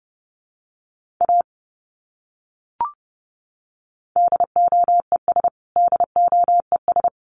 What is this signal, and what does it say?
Morse code, heard as a keyed tone, about 30 wpm, 700 Hz. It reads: A DOES DOES
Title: Top 200 Words Repeated 30wpm